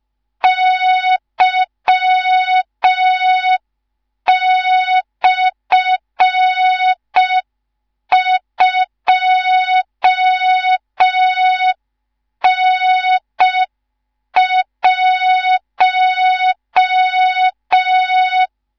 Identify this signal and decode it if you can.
Morse code, heard as a keyed tone, about 5 wpm, 3700 Hz. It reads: Y/2N1
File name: Morze-HA2MN / hang / 19 GYAKORLAT